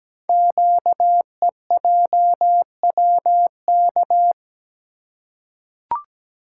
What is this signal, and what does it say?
Morse code, heard as a keyed tone, about 17 wpm, 700 Hz. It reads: QEJWK
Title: Code Groups - 5 Letters - 17wpm